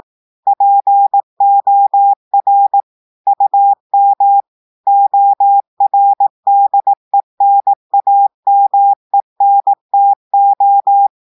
Morse code, heard as a keyed tone, about 18 wpm, 800 Hz. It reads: POR UM ORDENAMENTO
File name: audios / portugais / portugais texte en clair 1 vitesse 18 mots minutes